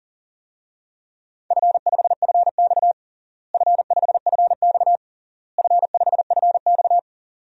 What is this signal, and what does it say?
Morse code, heard as a keyed tone, about 37 wpm, 700 Hz. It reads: F5FX F5FX F5FX
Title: DXCC-100 Call Signs - 4 Characters - Repeated 3x - 40wpm